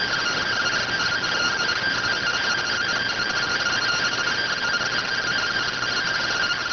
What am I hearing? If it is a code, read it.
Two-tone teleprinter signal, about 170 Hz shift.